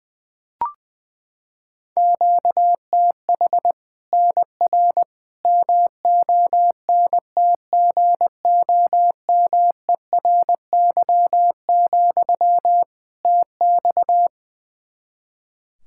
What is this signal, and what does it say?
Morse code, heard as a keyed tone, about 21 wpm, 700 Hz. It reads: QTH NR MONTGOMERY, TX
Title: Practice QSO - Exchange 1 - Beginning - 20wpm